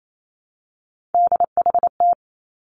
Morse code, heard as a keyed tone, about 28 wpm, 700 Hz. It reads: DHT
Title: Three Challenging Letter-Number Combinations 28wpm